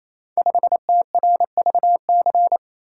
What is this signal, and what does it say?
Morse code, heard as a keyed tone, about 27 wpm, 700 Hz. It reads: 5TRVC
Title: Code Groups - 5 Letter:Numbers - 28wpm